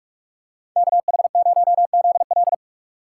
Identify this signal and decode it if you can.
Morse code, heard as a keyed tone, about 44 wpm, 700 Hz. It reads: KH0ZL
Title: USA Call Signs - 5 Characters - 2x Character Spacing - 45wpm